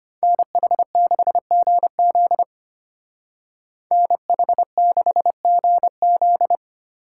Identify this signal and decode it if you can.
Morse code, heard as a keyed tone, about 32 wpm, 700 Hz. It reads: NH6GZ NH6GZ